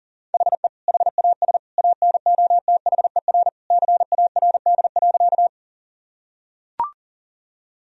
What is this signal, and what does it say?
Morse code, heard as a keyed tone, about 40 wpm, 700 Hz. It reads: HE HAS ANOTHER CARD.
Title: Sentences from Top 1000 Words - Review Part 4 - 40wpm